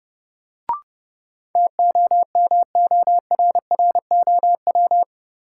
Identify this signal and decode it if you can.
Morse code, heard as a keyed tone, about 30 wpm, 700 Hz. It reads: TOMORROW
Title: Sentences from Top 1100 Words - New Word Drill Alt - 30wpm